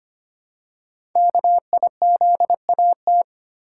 Morse code, heard as a keyed tone, about 25 wpm, 700 Hz. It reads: KIZAT
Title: Code Groups - 5 Letters - 25wpm